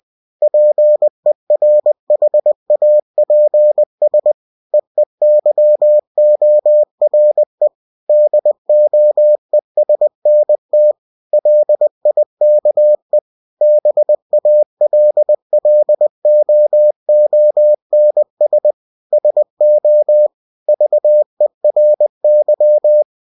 Morse code, heard as a keyed tone, about 20 wpm, 600 Hz. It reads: PERHAPS EEYORE DOESNT LIKE BALLOONS SO VERY
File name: cw-books / winnie-the-pooh / winnie-20wpm-page-0053